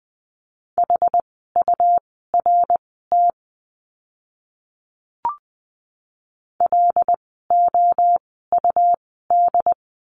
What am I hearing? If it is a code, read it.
Morse code, heard as a keyed tone, about 20 wpm, spaced as for 15 wpm, 700 Hz. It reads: HURT LOUD
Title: 101 Lesson 17f - Letter u - Words 20wpm